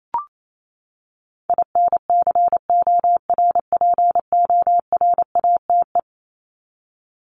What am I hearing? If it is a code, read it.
Morse code, heard as a keyed tone, about 28 wpm, 700 Hz. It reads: INCORPORATE
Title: Top 1751-2000 Words 28wpm